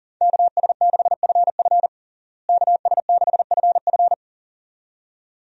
Morse code, heard as a keyed tone, about 40 wpm, 700 Hz. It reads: KS6FF KS6FF
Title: USA Call Signs - 5 Characters - Repeated 2x - 40wpm